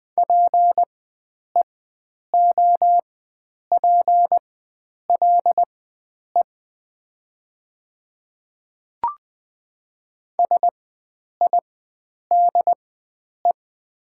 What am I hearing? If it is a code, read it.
Morse code, heard as a keyed tone, about 20 wpm, spaced as for 9 wpm, 700 Hz. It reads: PEOPLE SIDE
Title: 4x Character Spacing - Top 100 Words - 20wpm